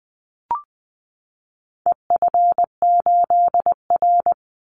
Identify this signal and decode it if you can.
Morse code, heard as a keyed tone, about 20 wpm, 700 Hz. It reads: EF8R
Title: Call Signs Repeated 2x - 20wpm